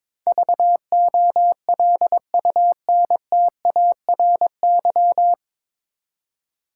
Morse code, heard as a keyed tone, about 22 wpm, 700 Hz. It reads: VOLUNTARY
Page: Morse Code Ninja practice 22wpm